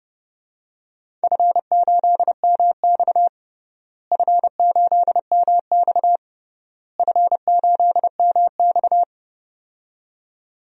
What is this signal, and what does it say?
Morse code, heard as a keyed tone, about 30 wpm, 700 Hz. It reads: F8MX F8MX F8MX
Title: DXCC-100 Call Signs - 4 Characters - Repeated 3x - 30wpm